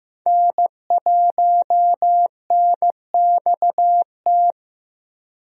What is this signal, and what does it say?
Morse code, heard as a keyed tone, about 15 wpm, 700 Hz. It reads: N1NXT
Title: USA Call Signs - 5 Characters - Repeated 2x - 15wpm